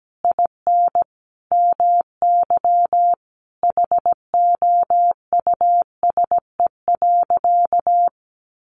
Morse code, heard as a keyed tone, about 17 wpm, 700 Hz. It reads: IN MY HOUSE.
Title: Sentences from Top 1100 Words 17wpm